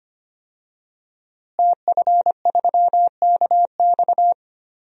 Morse code, heard as a keyed tone, about 25 wpm, 700 Hz. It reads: TF3KX
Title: DXCC-100 Call Signs - 5 Characters - Repeated 2x - 25wpm